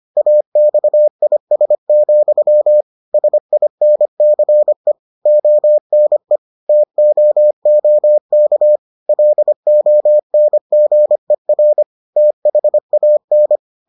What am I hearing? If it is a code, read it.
Morse code, heard as a keyed tone, about 25 wpm, 600 Hz. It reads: AXIS, SINCE ONE TOOK LONGER THAN